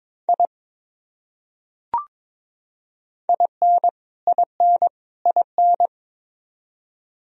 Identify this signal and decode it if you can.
Morse code, heard as a keyed tone, about 22 wpm, 700 Hz. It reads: I IN IN IN
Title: Top 100 Words Repeated 22wpm